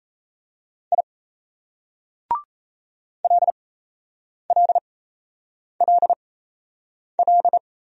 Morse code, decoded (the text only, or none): I LLLL